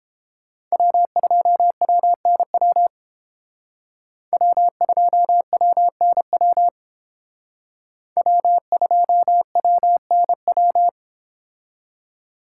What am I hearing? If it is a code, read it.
Morse code, heard as a keyed tone, about 29 wpm, 700 Hz. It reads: W2WNW W2WNW W2WNW